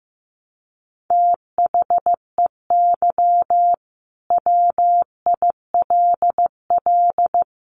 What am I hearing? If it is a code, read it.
Morse code, heard as a keyed tone, about 15 wpm, 700 Hz. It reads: THEY WILL
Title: Sentences from Top 1000 Words 15wpm